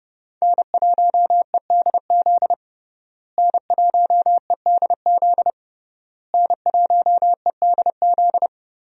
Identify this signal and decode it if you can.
Morse code, heard as a keyed tone, about 30 wpm, 700 Hz. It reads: N1EDZ N1EDZ N1EDZ
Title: USA Call Signs - 5 Characters - Repeated 3x - 30wpm